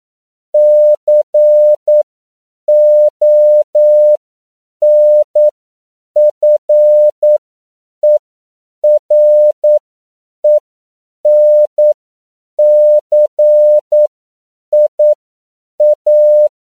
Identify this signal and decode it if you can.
Morse code, heard as a keyed tone, about 9 wpm, spaced as for 7 wpm, 600 Hz. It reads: CONFERENCIA
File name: CW / 10PPM